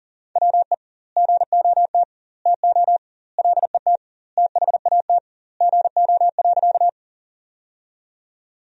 Morse code, heard as a keyed tone, about 40 wpm, 700 Hz. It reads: WE GOT TO LET THAT GO.
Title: 2x Word-Spacing - Sentences from Top 300 Words - 40wpm